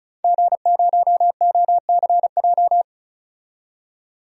Morse code, heard as a keyed tone, about 34 wpm, 700 Hz. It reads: G0OCJ
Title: DXCC-100 Call Signs - 5 Characters - 35wpm